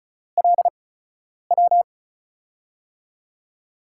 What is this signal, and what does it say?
Morse code, heard as a keyed tone, about 36 wpm, 700 Hz. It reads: L W
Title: USA Call Signs - 6 Characters - 8x Character Spacing - 35wpm